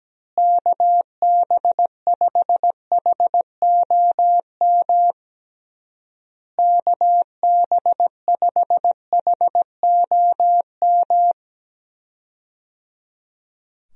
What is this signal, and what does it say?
Morse code, heard as a keyed tone, about 17 wpm, 700 Hz. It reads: KB5HOM KB5HOM
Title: USA Call Signs - 6 Characters - Repeated 3x - 17wpm